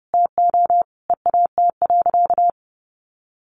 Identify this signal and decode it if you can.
Morse code, heard as a keyed tone, about 30 wpm, 700 Hz. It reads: TO EAT.
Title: Sentences from Top 1300 Words 30wpm